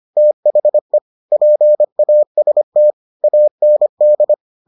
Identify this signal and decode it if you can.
Morse code, heard as a keyed tone, about 25 wpm, 600 Hz. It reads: THE PAST AND